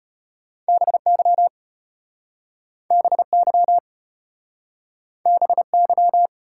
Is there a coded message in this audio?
Yes: Morse code from 0.7 to 6.3 s, heard as a keyed tone, about 33 wpm, 700 Hz. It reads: BY BY BY